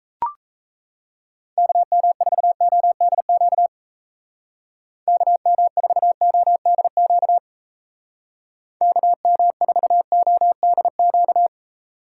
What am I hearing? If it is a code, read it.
Morse code, heard as a keyed tone, about 34 wpm, 700 Hz. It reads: KM4ODQ KM4ODQ KM4ODQ
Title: Speed Racing - USA 2x3 Call Signs 28wpm